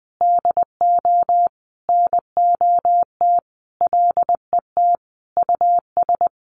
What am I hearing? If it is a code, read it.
Morse code, heard as a keyed tone, about 20 wpm, 700 Hz. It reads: DO NOT LET US
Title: Sentences from Top 1100 Words 20wpm